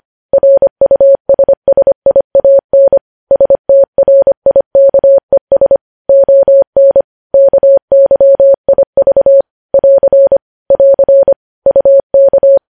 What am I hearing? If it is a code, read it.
Morse code, heard as a keyed tone, about 25 wpm, 550 Hz. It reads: RUSSIAN STRIKES ON KYIV <AR> <AR> UK
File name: podcast / News headlines-1x-25x25